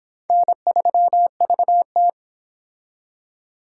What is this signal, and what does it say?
Morse code, heard as a keyed tone, about 25 wpm, 700 Hz. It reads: N3VT